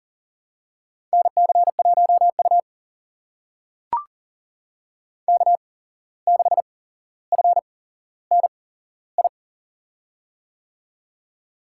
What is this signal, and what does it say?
Morse code, heard as a keyed tone, about 38 wpm, spaced as for 16 wpm, 700 Hz. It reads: NC1U K6FNI
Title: USA Call Signs - 8x Character Spacing - 40wpm